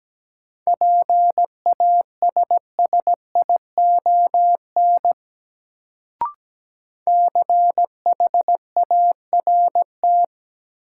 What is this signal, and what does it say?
Morse code, heard as a keyed tone, about 17 wpm, 700 Hz. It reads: PASSION CHART
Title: Sentences from Top 1300 Words - New Word Drill Alt - 17wpm